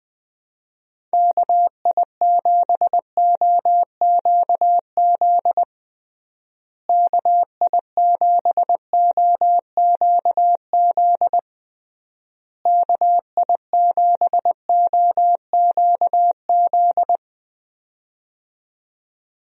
Morse code, heard as a keyed tone, about 20 wpm, 700 Hz. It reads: KI7OQZ KI7OQZ KI7OQZ
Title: USA Call Signs - 6 Characters - Repeated 3x - 20wpm